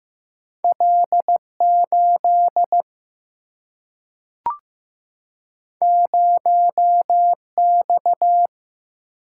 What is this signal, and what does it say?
Morse code, heard as a keyed tone, about 15 wpm, 700 Hz. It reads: L 8 0 X
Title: Single-Letter Single-Number 15wpm